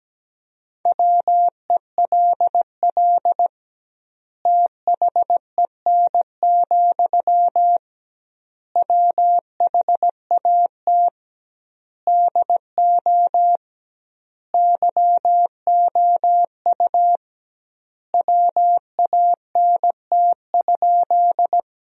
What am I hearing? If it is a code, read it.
Morse code, heard as a keyed tone, about 17 wpm, 700 Hz. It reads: WELL THEN, WHAT DO YOU WANT?
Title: 2x Word-Spacing - Sentences from Top 200 Words - 17wpm